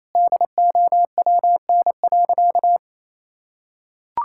Morse code, heard as a keyed tone, about 28 wpm, 700 Hz. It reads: DOWN.
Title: Sentences from Top 1600 Words 28wpm